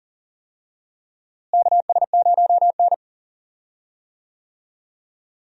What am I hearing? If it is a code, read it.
Morse code, heard as a keyed tone, about 40 wpm, 700 Hz. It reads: KS0N
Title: USA Call Signs - 4 Characters - Repeated 2x - 40wpm